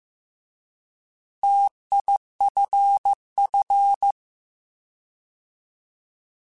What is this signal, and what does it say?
Morse code, heard as a keyed tone, about 15 wpm, 800 Hz. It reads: TIFF
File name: T Prefixes / T mp3s